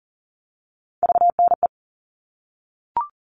Morse code, heard as a keyed tone, about 40 wpm, 700 Hz. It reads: VNE
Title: All Cut Numbers - 3-Digit 40wpm